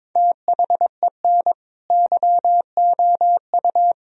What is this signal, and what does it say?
Morse code, heard as a keyed tone, about 22 wpm, 700 Hz. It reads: THEN YOU
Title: Sentences from Top 1000 Words - Review Part 1 - 22wpm